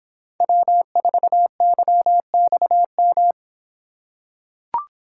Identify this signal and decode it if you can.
Morse code, heard as a keyed tone, about 26 wpm, 700 Hz. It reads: W4YXM